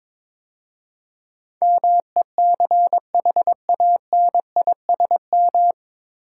Morse code, heard as a keyed tone, about 22 wpm, 700 Hz. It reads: MECHANISM
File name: 9 Letter Words - First 3 Letters Given - 22wpm